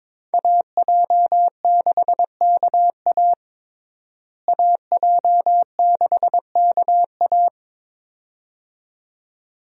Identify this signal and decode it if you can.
Morse code, heard as a keyed tone, about 21 wpm, 700 Hz. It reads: AJ6KA AJ6KA
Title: USA Call Signs - 5 Characters - Repeated 2x - 22wpm